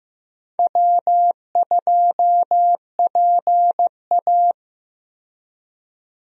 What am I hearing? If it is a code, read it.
Morse code, heard as a keyed tone, about 15 wpm, 700 Hz. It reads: W2PA
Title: USA Call Signs - 4 Characters - 15wpm